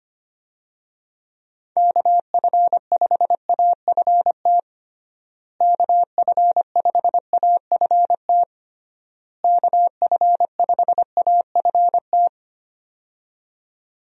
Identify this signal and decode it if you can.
Morse code, heard as a keyed tone, about 25 wpm, 700 Hz. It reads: KF5AFT KF5AFT KF5AFT
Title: USA Call Signs - 6 Characters - Repeated 3x - 25wpm